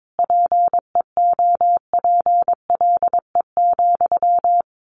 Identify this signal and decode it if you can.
Morse code, heard as a keyed tone, about 22 wpm, 700 Hz. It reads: PEOPLE,
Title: A Sound Of Thunder 22wpm